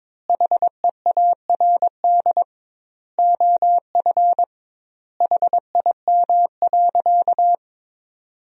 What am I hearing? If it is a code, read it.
Morse code, heard as a keyed tone, about 22 wpm, 700 Hz. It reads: HEARD OF HIM.
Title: 2x Word-Spacing - Sentences from Top 400 Words - 22wpm